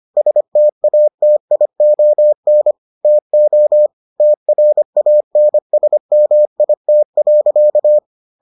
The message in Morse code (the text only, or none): STATION TO TRANSMIT.